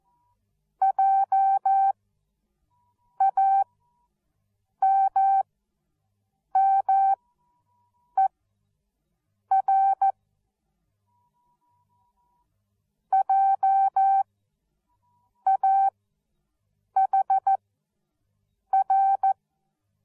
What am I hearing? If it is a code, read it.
Morse code, heard as a keyed tone, about 14 wpm, spaced as for 6 wpm, 780 Hz. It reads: JAMMER JAHR
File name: Onlinelehrgang / morsen / DJ4UF Morsen13